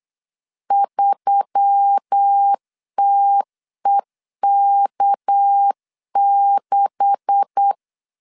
Morse code, heard as a keyed tone, about 8 wpm, 800 Hz. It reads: 3TEK6
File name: CW / DET / 1 9 1500C S B 8wpm 5624